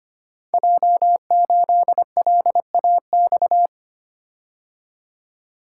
Morse code, heard as a keyed tone, about 25 wpm, 700 Hz. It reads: J8LAX
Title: Code Groups - 5 Letter:Numbers - 25wpm